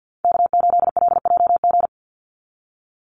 Morse code, heard as a keyed tone, about 50 wpm, 700 Hz. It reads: K8LJG
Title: NA SOTA Chaser Callsigns 50wpm